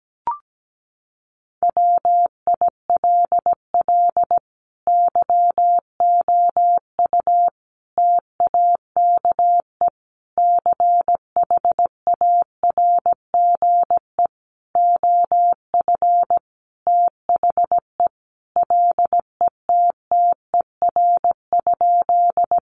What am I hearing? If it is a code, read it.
Morse code, heard as a keyed tone, about 17 wpm, 700 Hz. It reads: WILL YOU TAKE CHARGE OF THE LETTER?
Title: Sentences from Top 900 Words 17wpm